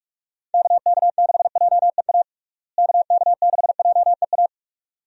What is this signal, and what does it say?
Morse code, heard as a keyed tone, about 45 wpm, 700 Hz. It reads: KK6JEA KK6JEA
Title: USA Call Signs - 6 Characters - Repeated 2x - 45wpm